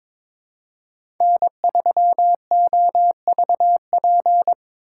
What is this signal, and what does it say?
Morse code, heard as a keyed tone, about 23 wpm, 700 Hz. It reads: N3OVP
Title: USA Call Signs - 5 Characters - 3x Character Spacing - 22wpm